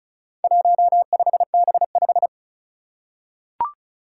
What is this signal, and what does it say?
Morse code, heard as a keyed tone, about 36 wpm, 700 Hz. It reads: 15B5
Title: Four Challenging Letter-Number Combinations 35wpm